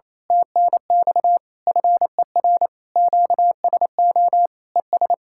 Morse code, heard as a keyed tone, about 26 wpm, 700 Hz. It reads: TNX FER QSO ES
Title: Speed-Racing Common QSO Phrases Encore - 28wpm